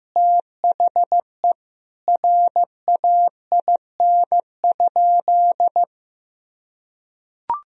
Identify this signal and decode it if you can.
Morse code, heard as a keyed tone, about 16 wpm, 700 Hz. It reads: THE RAIN?
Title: Sentences from Top 1000 Words - Review Part 1 - 15wpm